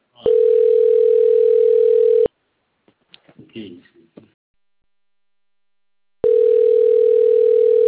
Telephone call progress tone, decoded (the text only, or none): ringback tone